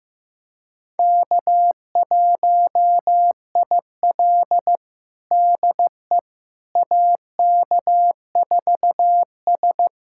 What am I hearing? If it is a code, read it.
Morse code, heard as a keyed tone, about 15 wpm, 700 Hz. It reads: K1IL DE AK4S
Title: USA Callsign DE Callsign - 4 Characters - 15wpm